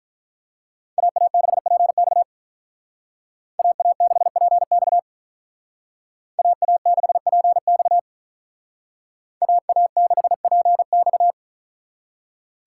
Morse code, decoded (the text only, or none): AA6PX AA6PX AA6PX AA6PX